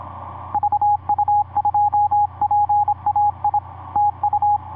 Morse code, heard as a keyed tone, about 26 wpm, 830 Hz, with radio noise. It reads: VU2PAI TU
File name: vu2pai ik5pwq etc 17m cw 16jan2010 1003